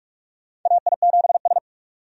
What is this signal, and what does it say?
Morse code, heard as a keyed tone, about 45 wpm, 700 Hz. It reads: AI7S
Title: USA Call Signs - 4 Characters - 45wpm